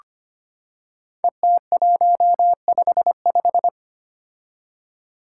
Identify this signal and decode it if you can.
Morse code, heard as a keyed tone, about 25 wpm, 700 Hz. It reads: ET155
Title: Five Challenging Letter-Number Combinations 25wpm